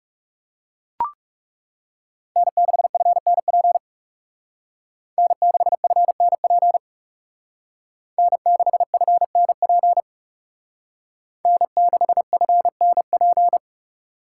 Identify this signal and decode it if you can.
Morse code, heard as a keyed tone, about 37 wpm, 700 Hz. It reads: N6FNP N6FNP N6FNP N6FNP
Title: Speed Racing - USA 1x3 Call Signs 30wpm